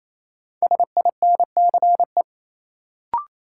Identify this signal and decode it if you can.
Morse code, heard as a keyed tone, about 28 wpm, 700 Hz. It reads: SINCE
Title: Sentences from Top 300 Words - New Word Drill - 28wpm